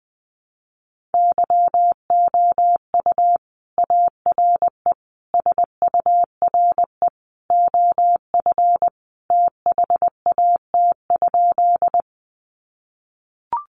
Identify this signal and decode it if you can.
Morse code, heard as a keyed tone, about 20 wpm, 700 Hz. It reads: YOU ARE SURE OF THAT?
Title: Sentences from Top 300 Words 20wpm